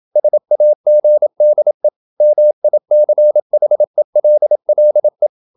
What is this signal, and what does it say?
Morse code, heard as a keyed tone, about 27 wpm, 600 Hz. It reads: SAGDE MICHELLE
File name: Ferie QRQ 27wpm